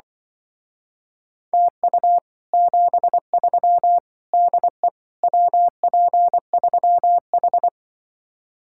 Morse code, heard as a keyed tone, about 24 wpm, 700 Hz. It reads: TU 73 DE WP3H
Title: Speed-Racing Common QSO Phrases Encore - 20wpm